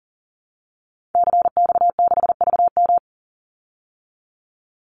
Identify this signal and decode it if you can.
Morse code, heard as a keyed tone, about 40 wpm, 700 Hz. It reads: CX6VM
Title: Call Signs Repeated 2x - 40wpm